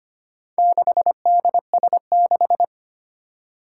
Morse code, heard as a keyed tone, about 25 wpm, 700 Hz. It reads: 6DS6
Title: Four Challenging Letter-Number Combinations 25wpm